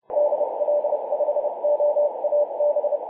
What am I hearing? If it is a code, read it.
Morse code, heard as a keyed tone, about 29 wpm, 590 Hz, with noise. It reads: DL5OAB
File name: 2013/Aug/20 0905z E44PM 21.026MHz CW